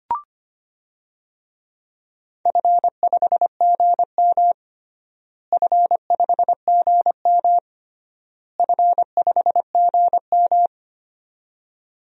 Morse code, heard as a keyed tone, about 25 wpm, 700 Hz. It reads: F5GM F5GM F5GM
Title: DXCC-100 Call Signs - 4 Characters - Repeated 3x - 25wpm